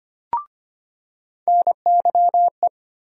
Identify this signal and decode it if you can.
Morse code, heard as a keyed tone, about 25 wpm, 700 Hz. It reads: NYE
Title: USA Call Sign Postfixes - 3 Letters 25wpm